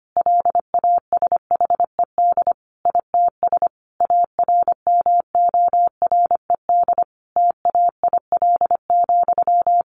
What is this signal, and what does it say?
Morse code, heard as a keyed tone, about 25 wpm, 700 Hz. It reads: LASHED ITS ARMORED TAIL,
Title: A Sound Of Thunder 25wpm